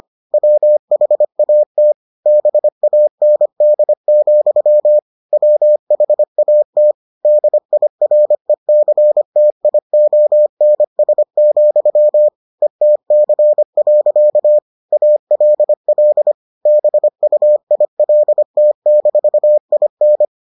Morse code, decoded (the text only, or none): WHAT BAND, WHAT DIRECTIONS, ETC. ALL BUILT-IN